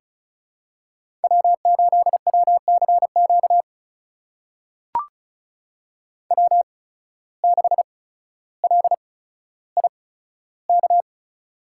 Morse code, heard as a keyed tone, about 36 wpm, spaced as for 15 wpm, 700 Hz. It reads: W8WCQ W6LIK
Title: USA Call Signs - 5 Characters - 8x Character Spacing - 35wpm